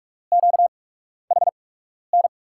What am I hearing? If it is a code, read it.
Morse code, heard as a keyed tone, about 44 wpm, spaced as for 12 wpm, 700 Hz. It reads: QHN
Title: USA Call Signs - 8x Character Spacing - 45wpm